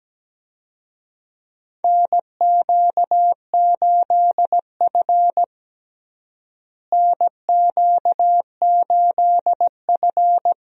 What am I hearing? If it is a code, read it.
Morse code, heard as a keyed tone, about 17 wpm, 700 Hz. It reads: NQ8F NQ8F
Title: USA Call Signs - 4 Characters - Repeated 3x - 17wpm